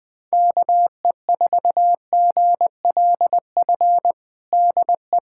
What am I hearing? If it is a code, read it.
Morse code, heard as a keyed tone, about 20 wpm, 700 Hz. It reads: KE4GLF DE
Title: USA Callsign DE Callsign 20wpm